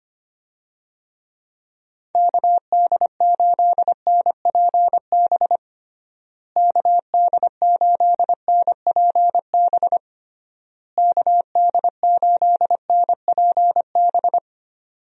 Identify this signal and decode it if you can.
Morse code, heard as a keyed tone, about 25 wpm, 700 Hz. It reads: KD8NPB KD8NPB KD8NPB
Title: USA Call Signs - 6 Characters - Repeated 3x - 25wpm